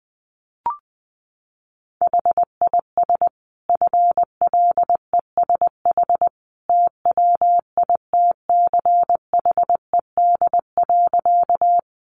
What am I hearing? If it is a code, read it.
Morse code, heard as a keyed tone, about 20 wpm, 700 Hz. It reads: HIS FLESH TWITCHED.
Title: A Sound Of Thunder 20wpm